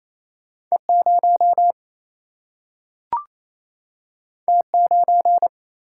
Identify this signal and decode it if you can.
Morse code, heard as a keyed tone, about 28 wpm, 700 Hz. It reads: E0 T9
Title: Single-Letter Single-Number 28wpm